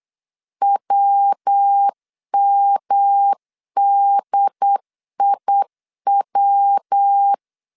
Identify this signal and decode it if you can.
Morse code, heard as a keyed tone, about 8 wpm, 800 Hz. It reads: WMDIW
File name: CW / DET / 1 3 1500C S B 8wpm 4141